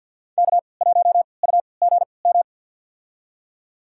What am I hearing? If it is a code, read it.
Morse code, heard as a keyed tone, about 47 wpm, spaced as for 28 wpm, 700 Hz. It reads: K1UGM
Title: USA Call Signs - 5 Characters - 3x Character Spacing - 50wpm